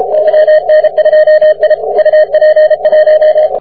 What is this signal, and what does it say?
Morse code, heard as a keyed tone, about 33 wpm, 620 Hz. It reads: YN2I UP 1
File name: wp / yn2i 40m